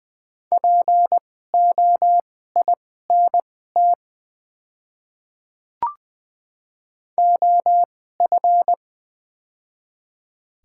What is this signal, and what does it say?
Morse code, heard as a keyed tone, about 20 wpm, spaced as for 14 wpm, 700 Hz. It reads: POINT OF
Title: 2x Character Spacing - Top 200 Words - 20wpm